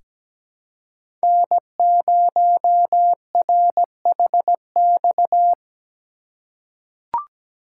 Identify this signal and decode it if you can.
Morse code, heard as a keyed tone, about 17 wpm, 700 Hz. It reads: N0RHX
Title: USA Call Signs - 5 Characters - 3x Character Spacing - 17wpm